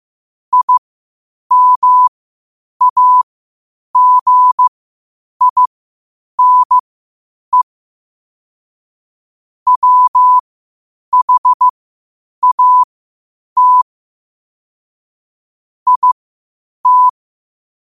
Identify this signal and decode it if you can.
Morse code, heard as a keyed tone, about 15 wpm, spaced as for 8 wpm, 1000 Hz. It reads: IMAGINE WHAT IT
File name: qotdfiles / 8wpm